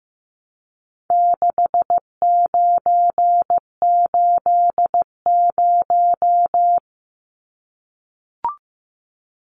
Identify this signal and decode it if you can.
Morse code, heard as a keyed tone, about 15 wpm, 700 Hz. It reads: 6980
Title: Numbers - 4-Digit 15wpm